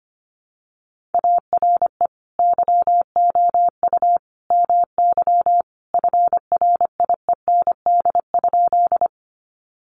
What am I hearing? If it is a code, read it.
Morse code, heard as a keyed tone, about 25 wpm, 700 Hz. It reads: ARE YOU MY FRIEND?
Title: Sentences from Top 300 Words 25wpm